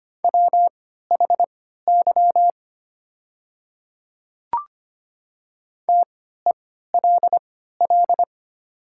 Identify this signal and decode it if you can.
Morse code, heard as a keyed tone, about 25 wpm, spaced as for 14 wpm, 700 Hz. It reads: WHY TELL
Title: 3x Character Spacing - Top 500 Words - 25wpm